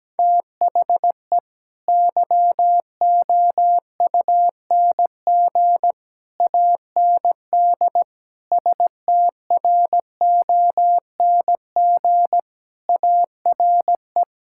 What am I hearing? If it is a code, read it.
Morse code, heard as a keyed tone, about 17 wpm, 700 Hz. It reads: THE YOUNG AND STRONG ARE
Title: Sentences from Top 1000 Words - Review Part 5 - 17wpm